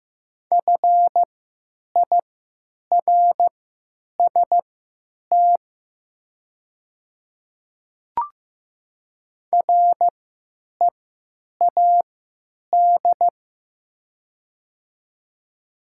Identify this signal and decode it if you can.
Morse code, heard as a keyed tone, about 15 wpm, spaced as for 8 wpm, 700 Hz. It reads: FIRST READ
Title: 3x Character Spacing - Top 500 Words - 15wpm